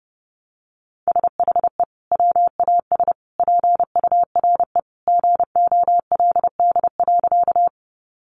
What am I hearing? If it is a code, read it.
Morse code, heard as a keyed tone, about 30 wpm, 700 Hz. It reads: SHE WAS PURE GOLD.
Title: Sentences from Top 1500 Words 30wpm